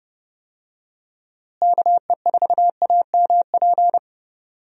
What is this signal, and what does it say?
Morse code, heard as a keyed tone, about 30 wpm, 700 Hz. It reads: KE4AMP
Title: USA Call Signs - 6 Characters - Repeated 2x - 30wpm